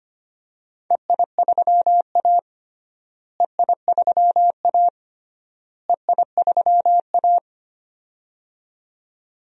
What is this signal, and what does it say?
Morse code, heard as a keyed tone, about 25 wpm, 700 Hz. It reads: EI3A EI3A EI3A
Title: DXCC-100 Call Signs - 4 Characters - Repeated 3x - 25wpm